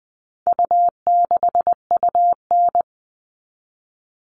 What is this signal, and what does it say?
Morse code, heard as a keyed tone, about 20 wpm, 700 Hz. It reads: U6UN
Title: All Cut Numbers - 4-Digit 20wpm